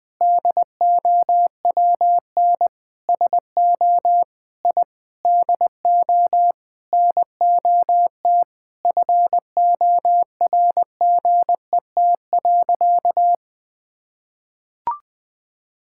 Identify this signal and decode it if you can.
Morse code, heard as a keyed tone, about 20 wpm, 700 Hz. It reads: DOWN SO I DO NOT FORGET.
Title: Sentences with 4-Word Phrases - Spoken Beginning - 20wpm